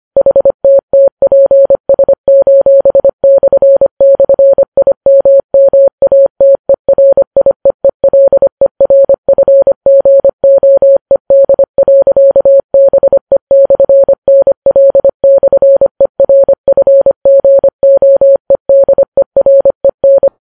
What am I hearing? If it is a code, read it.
Morse code, heard as a keyed tone, about 25 wpm, 550 Hz. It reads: HTTPS://IMMATERIEELERFGOED.BE/NL/ERFGOEDEREN